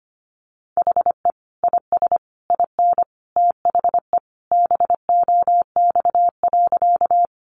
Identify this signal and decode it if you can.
Morse code, heard as a keyed tone, about 25 wpm, 700 Hz. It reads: HE IS IN THE BOX.
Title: Sentences from Top 500 Words 25wpm